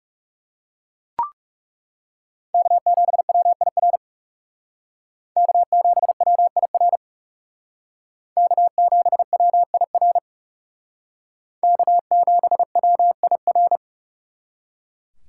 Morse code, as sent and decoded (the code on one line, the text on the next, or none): -.- --... .-- .. .-. / -.- --... .-- .. .-. / -.- --... .-- .. .-. / -.- --... .-- .. .-.
K7WIR K7WIR K7WIR K7WIR